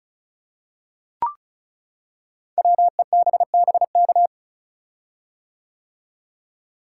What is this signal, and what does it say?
Morse code, heard as a keyed tone, about 36 wpm, 690 Hz. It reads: WEBBK
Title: Code Groups - 5 Letter:Numbers - 35wpm